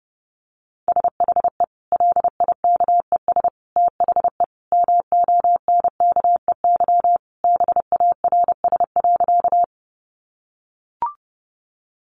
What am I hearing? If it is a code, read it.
Morse code, heard as a keyed tone, about 30 wpm, 700 Hz. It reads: SHE LIKES THE MONKEY BARS.
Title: Easy Phrases 30wpm